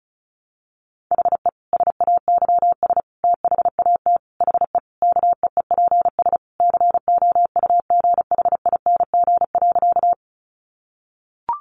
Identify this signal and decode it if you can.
Morse code, heard as a keyed tone, about 34 wpm, 700 Hz. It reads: HE SAYS THAT HE KEEPS COUGHING.
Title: Additional Easy Phrases 35wpm